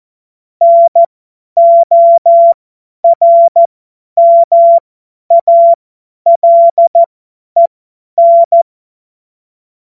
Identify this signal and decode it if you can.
Morse code, heard as a keyed tone, about 14 wpm, spaced as for 10 wpm, 680 Hz. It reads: NORMALEN
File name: Morsekurs / Witze / Witz 27. April 2021, 50 F70